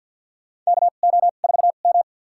Morse code, heard as a keyed tone, about 50 wpm, spaced as for 36 wpm, 700 Hz. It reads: KO4M
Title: USA Call Signs - 4 Characters - 2x Character Spacing - 50wpm